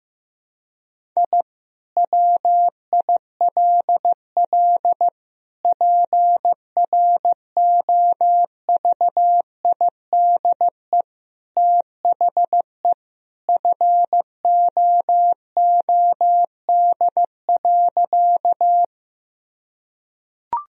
Morse code, heard as a keyed tone, about 15 wpm, 700 Hz. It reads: I WILL PROVIDE THE FOOD.
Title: Sentences from Top 1000 Words - Review Part 3 - 15wpm